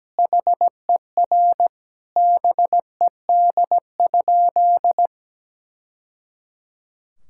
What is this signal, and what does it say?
Morse code, heard as a keyed tone, about 17 wpm, 700 Hz. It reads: HER BED?
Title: Sentences from Top 1000 Words - Review Part 6 - 17wpm